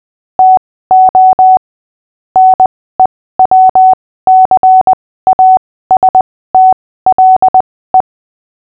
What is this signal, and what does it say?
Morse code, heard as a keyed tone, about 20 wpm, spaced as for 15 wpm, 750 Hz. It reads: TO NEWCASTLE
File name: cw / fists / Short-Story-5-15wpm